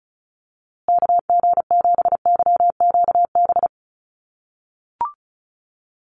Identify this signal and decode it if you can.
Morse code, heard as a keyed tone, about 34 wpm, 700 Hz. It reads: KG7YQB
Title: USA Call Signs Repeated 2x 35wpm